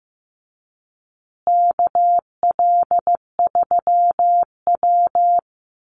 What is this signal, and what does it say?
Morse code, heard as a keyed tone, about 15 wpm, 700 Hz. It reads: KL3W
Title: Call Signs Repeated 3x - 15wpm